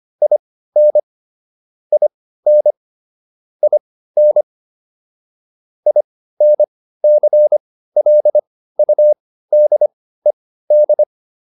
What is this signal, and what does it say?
Morse code, heard as a keyed tone, about 25 wpm, spaced as for 15 wpm, 600 Hz. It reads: IN IN IN INCLUDED
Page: intermediate-practice-files